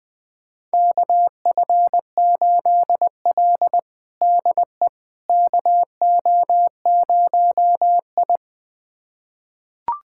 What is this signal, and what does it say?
Morse code, heard as a keyed tone, about 20 wpm, 700 Hz. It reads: KF8L DE KO0I
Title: USA Callsign DE Callsign - 4 Characters - 20wpm